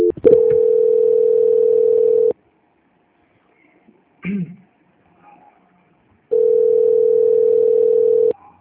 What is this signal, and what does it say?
Ringback tone (North America), 440 + 480 Hz.
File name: out- -2037- - - .405